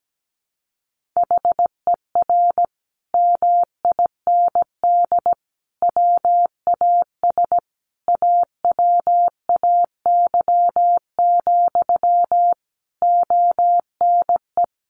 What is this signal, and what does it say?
Morse code, heard as a keyed tone, about 17 wpm, 700 Hz. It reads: HER MIND WAS AWAY, ONE
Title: Sentences from Top 1500 Words 17wpm